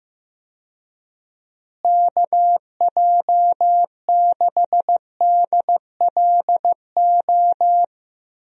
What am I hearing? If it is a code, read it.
Morse code, heard as a keyed tone, about 15 wpm, 700 Hz. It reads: KJ6DLO
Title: USA Call Signs - 6 Characters - Repeated 3x - 15wpm